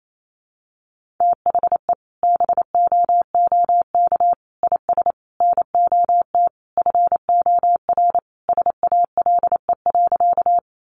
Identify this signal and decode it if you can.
Morse code, heard as a keyed tone, about 28 wpm, 700 Hz. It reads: THE BOOK IS NOT FOR SALE.
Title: Sentences from Top 1400 Words 28wpm